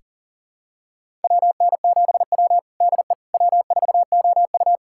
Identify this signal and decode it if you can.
Morse code, heard as a keyed tone, about 40 wpm, 700 Hz. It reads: WN7W DE W4OU
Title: USA Callsign DE Callsign - 4 Characters - 40wpm